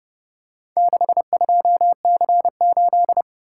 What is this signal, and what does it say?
Morse code, heard as a keyed tone, about 30 wpm, 700 Hz. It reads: 62C8